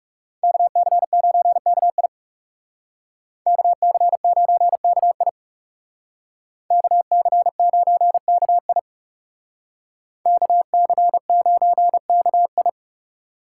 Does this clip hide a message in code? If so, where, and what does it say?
Yes: Morse code from 0.4 to 12.7 s, heard as a keyed tone, about 40 wpm, 700 Hz. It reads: KC9KI KC9KI KC9KI KC9KI